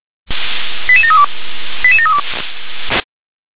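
Five-tone selective call (ZVEI), heard as quick runs of five tones, tones of about 70 ms. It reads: 88732 88732
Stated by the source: ZVEI five-tone selective call 88150 received by SDR.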